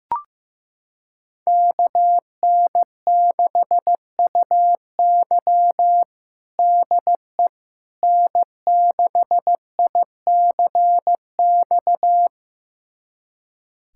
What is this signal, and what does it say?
Morse code, heard as a keyed tone, about 15 wpm, 700 Hz. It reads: KN6UY DE N6ICX
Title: USA Callsign DE Callsign - 5 Characters - 15wpm